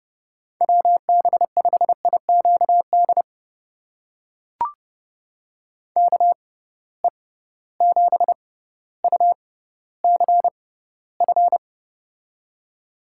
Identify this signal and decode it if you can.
Morse code, heard as a keyed tone, about 30 wpm, spaced as for 15 wpm, 700 Hz. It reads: WB5IQD KE7UCF
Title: USA Call Signs - 6 Characters - 6x Character Spacing - 30wpm